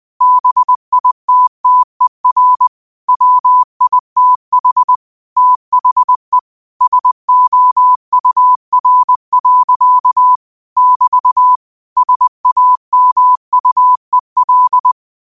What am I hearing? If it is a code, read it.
Morse code, heard as a keyed tone, about 20 wpm, 1000 Hz. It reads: BITTER WITH THE SOUR. <BT> SAMUEL
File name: qotdfiles / 20wpm